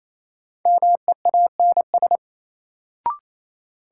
Morse code, heard as a keyed tone, about 28 wpm, 700 Hz. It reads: MEANS
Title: Sets of 5 Words - New Word Drill - 28wpm